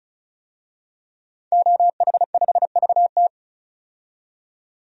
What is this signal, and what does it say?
Morse code, heard as a keyed tone, about 36 wpm, 700 Hz. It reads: OH5VT
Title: DXCC-100 Call Signs - 5 Characters - 35wpm